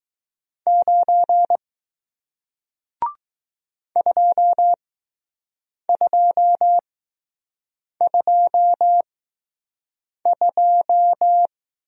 Morse code, heard as a keyed tone, about 20 wpm, spaced as for 6 wpm, 700 Hz. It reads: 9 2222